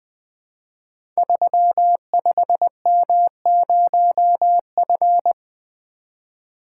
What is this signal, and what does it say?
Morse code, heard as a keyed tone, about 21 wpm, 700 Hz. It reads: 35M0F
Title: Code Groups - 5 Letter:Numbers - 20wpm